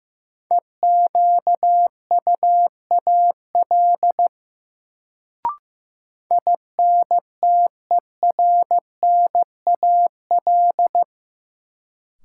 Morse code, heard as a keyed tone, about 15 wpm, 700 Hz. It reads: EQUAL INTERNAL
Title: Sentences from Top 1400 Words - New Word Drill Alt - 15wpm